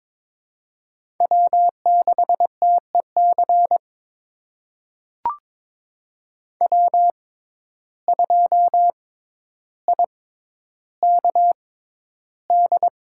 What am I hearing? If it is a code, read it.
Morse code, heard as a keyed tone, about 22 wpm, spaced as for 11 wpm, 700 Hz. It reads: W6TEC W2IKD
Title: USA Call Signs - 5 Characters - 6x Character Spacing - 22wpm